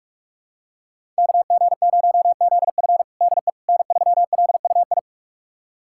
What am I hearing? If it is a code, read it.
Morse code, heard as a keyed tone, about 44 wpm, 700 Hz. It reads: KG0ZF DE N3LUI
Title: USA Callsign DE Callsign - 5 Characters - 45wpm